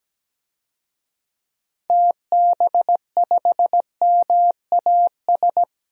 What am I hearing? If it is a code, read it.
Morse code, heard as a keyed tone, about 17 wpm, 700 Hz. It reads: TB5MAS
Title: DXCC-100 Call Signs - 6 Characters - 17wpm